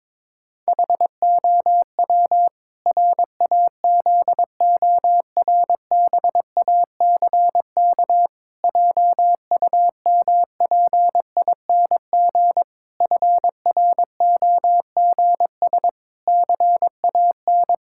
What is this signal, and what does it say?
Morse code, heard as a keyed tone, about 22 wpm, 700 Hz. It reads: HOW RAZORBACK JUMPING FROGS CAN
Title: Pangram Challenge v2 - 22wpm